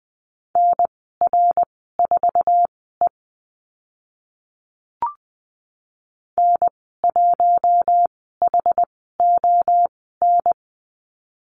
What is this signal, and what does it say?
Morse code, heard as a keyed tone, about 20 wpm, spaced as for 14 wpm, 700 Hz. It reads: NR4E N1HON
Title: 069 Lesson 12g - Letter d - Call Signs 20wpm